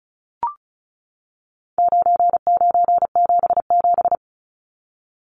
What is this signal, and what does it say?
Morse code, heard as a keyed tone, about 34 wpm, 700 Hz. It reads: 9977
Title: Numbers - 4-Digit 35wpm